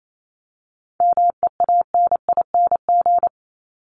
Morse code, heard as a keyed tone, about 28 wpm, 700 Hz. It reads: MEANING
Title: Top 751-1000 Words 28wpm